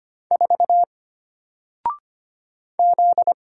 Morse code, heard as a keyed tone, about 25 wpm, 700 Hz. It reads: 4 Z
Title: Single Letter-Number 25wpm